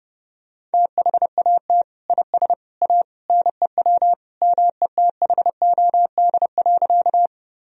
Morse code, heard as a keyed tone, about 30 wpm, 700 Hz. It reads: THAT IS A NEW METHOD.
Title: Sentences from Top 1000 Words - Review Part 3 - 30wpm